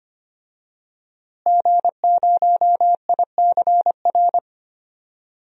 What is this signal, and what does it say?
Morse code, heard as a keyed tone, about 25 wpm, 700 Hz. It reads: G0ICR
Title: DXCC-100 Call Signs - 5 Characters - Repeated 3x - 25wpm